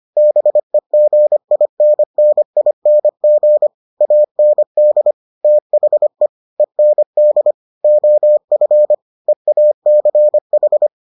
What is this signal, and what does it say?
Morse code, heard as a keyed tone, about 25 wpm, 600 Hz. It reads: BEGINNING AND THE END OF EACH